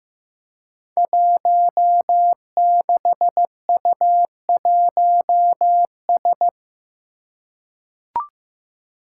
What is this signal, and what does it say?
Morse code, heard as a keyed tone, about 15 wpm, 700 Hz. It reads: 16U1S
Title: Five Challenging Letter-Number Combinations 15wpm